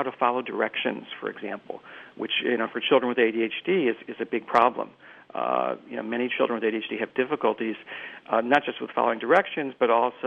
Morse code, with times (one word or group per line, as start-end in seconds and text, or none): none